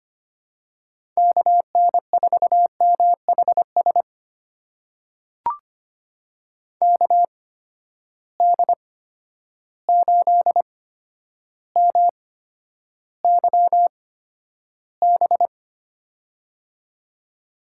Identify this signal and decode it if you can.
Morse code, heard as a keyed tone, about 25 wpm, spaced as for 11 wpm, 700 Hz. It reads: KN4MHS KD8MYB
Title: USA Call Signs - 6 Characters - 8x Character Spacing - 25wpm